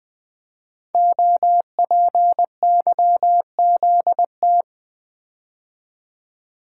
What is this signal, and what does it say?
Morse code, heard as a keyed tone, about 20 wpm, 700 Hz. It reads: OPYZT
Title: Code Groups - 5 Letters - 20wpm